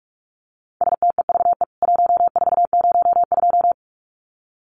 Morse code, heard as a keyed tone, about 44 wpm, 700 Hz. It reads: STEVE 1402
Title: CWT Exchange 45wpm